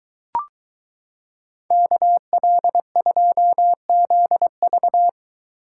Morse code, heard as a keyed tone, about 22 wpm, 700 Hz. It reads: KL2ZV